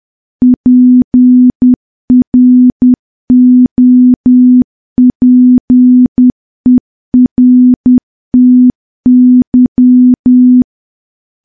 Morse code, heard as a keyed tone, about 10 wpm, 250 Hz. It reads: PROPERTY
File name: morse / morse-training-10WPM